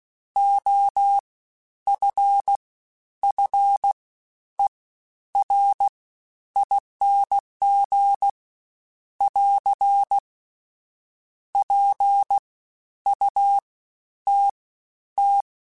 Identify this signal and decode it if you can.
Morse code, heard as a keyed tone, about 16 wpm, spaced as for 10 wpm, 800 Hz. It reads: OFFERING <AR> PUTT
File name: Special INGs-mp3s / Special INGs-mp3s / Fast ing @15